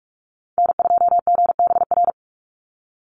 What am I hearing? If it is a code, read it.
Morse code, heard as a keyed tone, about 44 wpm, 700 Hz. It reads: N2GBR
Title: NA SOTA Chaser Callsigns 45wpm